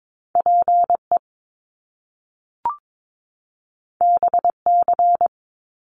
Morse code, heard as a keyed tone, about 22 wpm, 700 Hz. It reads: PE BC
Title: Canadian Provinces and Territories 22wpm